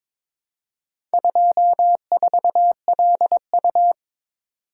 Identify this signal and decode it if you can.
Morse code, heard as a keyed tone, about 22 wpm, 700 Hz. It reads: 24LU
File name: practice / 22wpm / Four Letter-Number Combinations - 22wpm